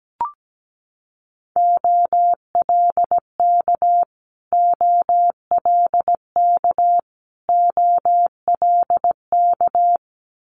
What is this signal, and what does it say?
Morse code, heard as a keyed tone, about 17 wpm, 700 Hz. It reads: OLK OLK OLK
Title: Multi-letter Phonemes Repeated 3x 17wpm